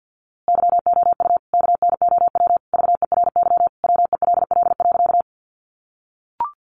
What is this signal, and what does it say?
Morse code, heard as a keyed tone, about 48 wpm, 700 Hz. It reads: YOU KNOW VERY WELL.
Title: A Sound Of Thunder 50wpm